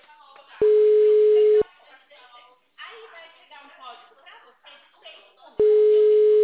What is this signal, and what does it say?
Ringback tone (Europe, 425 Hz), with speech.